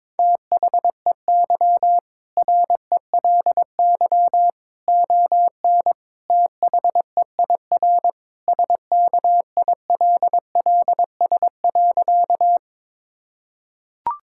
Morse code, heard as a keyed tone, about 22 wpm, 700 Hz. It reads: THEY RELY ON THEIR SKILLS.
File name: practice / 22wpm / Sentences with 2-Word Phrases - 22wpm